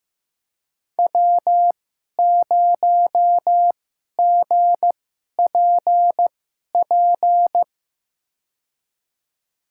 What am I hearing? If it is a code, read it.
Morse code, heard as a keyed tone, about 15 wpm, spaced as for 11 wpm, 700 Hz. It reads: W0GPP
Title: USA Call Signs - 5 Characters - 2x Character Spacing - 15wpm